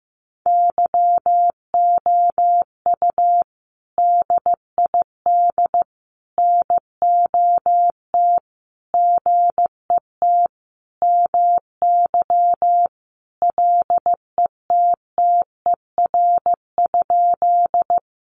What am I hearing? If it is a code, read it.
Morse code, heard as a keyed tone, about 15 wpm, 700 Hz. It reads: YOU DID NOT GET MY LETTER?
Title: Sentences from Top 600 Words 15wpm